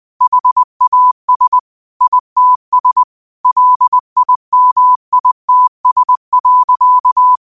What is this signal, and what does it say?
Morse code, heard as a keyed tone, about 20 wpm, 1000 Hz. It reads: HAS ITS LIMITS.